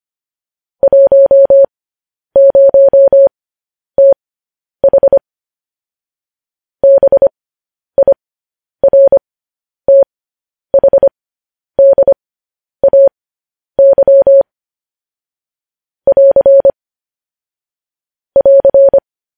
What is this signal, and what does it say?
Morse code, heard as a keyed tone, about 25 wpm, spaced as for 10 wpm, 550 Hz. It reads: 10TH BIRTHDAY <AR> <AR>
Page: news-headlines-in-morse-code-at-10-wpm